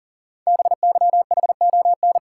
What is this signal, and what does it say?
Morse code, heard as a keyed tone, about 40 wpm, 690 Hz. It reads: BYHON
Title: Code Groups - 5 Letters - 40wpm